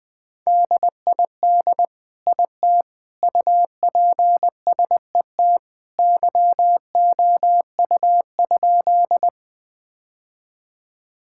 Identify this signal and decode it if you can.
Morse code, heard as a keyed tone, about 20 wpm, 700 Hz. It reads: DID IT UPSET YOU?
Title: Sentences from Top 1000 Words - Review Part 5 - 20wpm